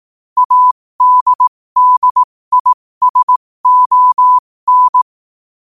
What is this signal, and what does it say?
Morse code, heard as a keyed tone, about 18 wpm, spaced as for 15 wpm, 1000 Hz. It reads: ADDISON
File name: qotdfiles / 15wpm